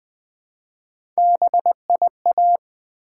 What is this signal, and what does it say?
Morse code, heard as a keyed tone, about 20 wpm, 700 Hz. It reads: BIA
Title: Build A Word - 1 Letter At A Time - 3 to 5-Letter Words - 20wpm